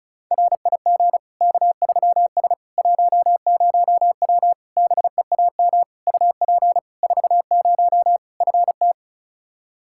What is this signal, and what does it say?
Morse code, heard as a keyed tone, about 36 wpm, 700 Hz. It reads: RIG K3S 10W BEAM UP 40 FT
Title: Practice QSO - Exchange 2 - Info - 35wpm